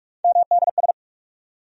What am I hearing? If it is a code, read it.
Morse code, heard as a keyed tone, about 47 wpm, 700 Hz. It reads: MDS
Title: USA Call Sign Postfixes - 3 Letters 45wpm